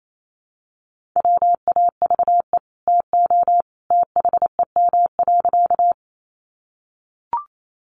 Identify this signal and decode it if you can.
Morse code, heard as a keyed tone, about 28 wpm, 700 Hz. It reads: WAVE TO THEM.
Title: Sentences from Top 1400 Words 28wpm